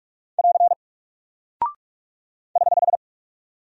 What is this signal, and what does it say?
Morse code, heard as a keyed tone, about 44 wpm, 700 Hz. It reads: <AR> <HH>
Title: Single Character (Letter:Number:Punctuation:Prosign) Alt 45wpm